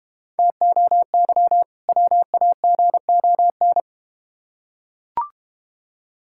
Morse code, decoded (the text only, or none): TOY WAGON